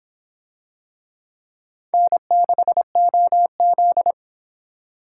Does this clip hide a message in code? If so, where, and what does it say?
Yes: Morse code from 1.9 to 4.1 s, heard as a keyed tone, about 22 wpm, 700 Hz. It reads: N6OZ